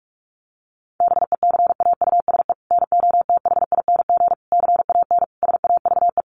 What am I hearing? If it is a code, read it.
Morse code, heard as a keyed tone, about 44 wpm, 700 Hz. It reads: BECAUSE NOTHING CAN SAVE
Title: Sentences from Top 1100 Words 45wpm